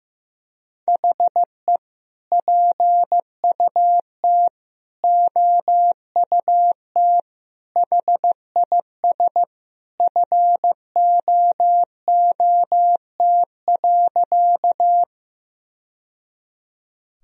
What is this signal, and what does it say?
Morse code, heard as a keyed tone, about 15 wpm, 700 Hz. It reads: HE PUT OUT HIS FOOT.
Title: Sentences from Top 1000 Words - Review Part 3 - 15wpm